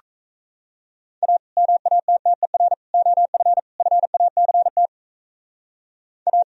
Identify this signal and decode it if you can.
Morse code, heard as a keyed tone, about 35 wpm, 700 Hz. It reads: A MATTER OF FACT A
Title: Speed-Racing - Sets of 4 Words 28wpm